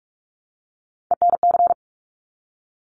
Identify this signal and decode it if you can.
Morse code, heard as a keyed tone, about 46 wpm, 690 Hz. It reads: ENC
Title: Top Three Letter Combinations in Words 45wpm